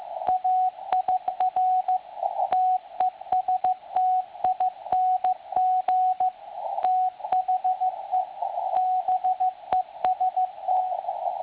Morse code, heard as a keyed tone, about 15 wpm, 740 Hz, with radio noise. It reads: AIR TESTING THE BEST
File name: attachments / websdr recording 7047.5kHz